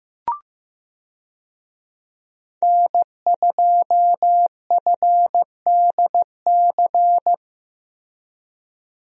Morse code, heard as a keyed tone, about 15 wpm, 700 Hz. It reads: N2FDC
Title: USA Call Signs - 5 Characters - Repeated 2x - 15wpm